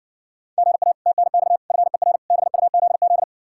Morse code, heard as a keyed tone, about 60 wpm, 700 Hz. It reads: GA TNX FER BUZZ